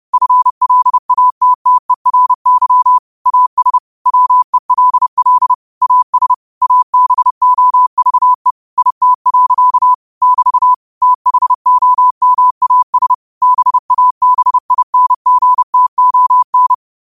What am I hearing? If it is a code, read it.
Morse code, heard as a keyed tone, about 30 wpm, 1000 Hz. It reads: FLATTERY AS WELL AS ABOVE IT. <BT> THOMAS BABINGTON
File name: qotdfiles / 30wpm